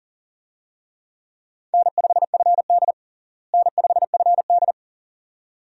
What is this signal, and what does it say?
Morse code, heard as a keyed tone, about 40 wpm, 700 Hz. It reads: N5FD N5FD
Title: USA Call Signs - 4 Characters - Repeated 2x - 40wpm